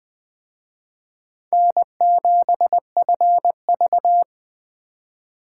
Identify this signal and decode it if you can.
Morse code, heard as a keyed tone, about 24 wpm, 700 Hz. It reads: N7FV
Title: Speed Racing - USA 1x2 Call Signs 20wpm